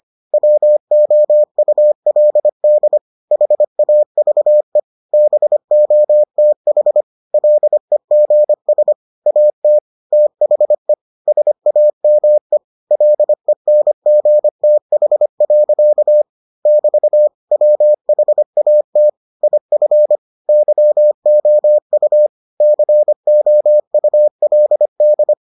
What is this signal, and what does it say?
Morse code, heard as a keyed tone, about 25 wpm, 600 Hz. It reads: WOULD HAVE BOTH LEGS AT THE SAME LENGTH. <BT> WHAT IF YOU COULD